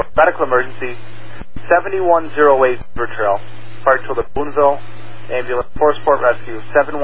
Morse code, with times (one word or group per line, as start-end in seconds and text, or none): none